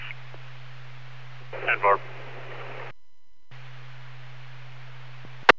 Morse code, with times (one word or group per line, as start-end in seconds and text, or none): none